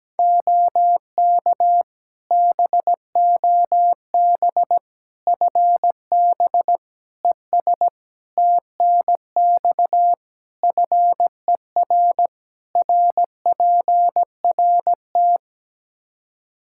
Morse code, heard as a keyed tone, about 17 wpm, 700 Hz. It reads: OK BOB FB ES TNX FER RPRT
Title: Practice QSO - Exchange 2 - Info - 17wpm